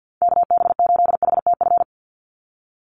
Morse code, heard as a keyed tone, about 50 wpm, 700 Hz. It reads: KB8HTF
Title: USA Call Signs 50wpm